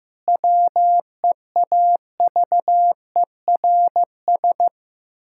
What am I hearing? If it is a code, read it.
Morse code, heard as a keyed tone, about 15 wpm, 700 Hz. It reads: WEAVERS
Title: Pangram Challenge v2 - 15wpm